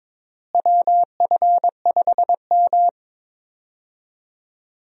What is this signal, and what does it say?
Morse code, heard as a keyed tone, about 22 wpm, 700 Hz. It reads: WF5M
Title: USA Call Signs - 4 Characters - Repeated 3x - 22wpm